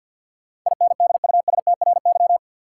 Morse code, heard as a keyed tone, about 51 wpm, 700 Hz. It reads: INDUSTRY
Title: Sentences from Top 1300 Words - New Word Drill Alt - 50wpm